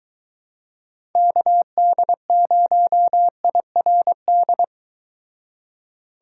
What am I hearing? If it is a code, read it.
Morse code, heard as a keyed tone, about 23 wpm, 700 Hz. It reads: KD0IRD